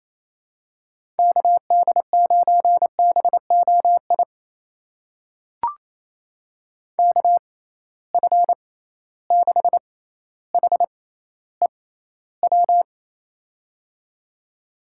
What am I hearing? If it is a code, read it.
Morse code, heard as a keyed tone, about 28 wpm, spaced as for 14 wpm, 700 Hz. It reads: KD9BOI KF6HEW
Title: USA Call Signs - 6 Characters - 6x Character Spacing - 28wpm